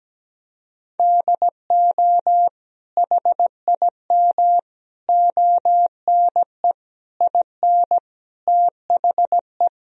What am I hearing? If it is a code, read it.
Morse code, heard as a keyed tone, about 17 wpm, 700 Hz. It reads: DO HIM ONE IN THE
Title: Sentences from Top 1000 Words - Review Part 1 - 17wpm